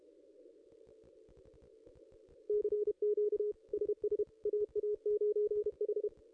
Morse code, heard as a keyed tone, about 28 wpm, 440 Hz. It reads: CQ SS AA9H
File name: contests / ss-cw-k5tr-op-n5rz -0500